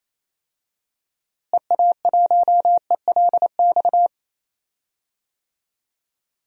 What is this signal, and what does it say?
Morse code, heard as a keyed tone, about 28 wpm, 700 Hz. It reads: EA1ELX
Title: DXCC-100 Call Signs - 6 Characters - Repeated 2x - 28wpm